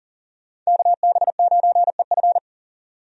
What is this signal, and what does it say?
Morse code, heard as a keyed tone, about 40 wpm, 700 Hz. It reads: KB9EF
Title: USA Call Signs - 5 Characters - 40wpm